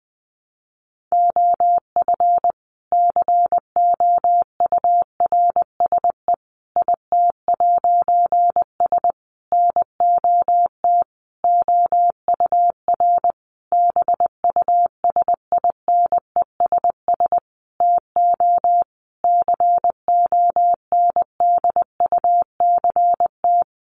Morse code, heard as a keyed tone, about 20 wpm, 700 Hz. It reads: OF COURSE IT'S NOT OUR BUSINESS TO CONDUCT
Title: A Sound Of Thunder 20wpm